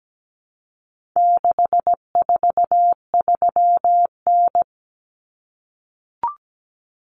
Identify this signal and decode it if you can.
Morse code, heard as a keyed tone, about 17 wpm, 700 Hz. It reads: 643N
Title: Common Cut Numbers - 4-Digit 17wpm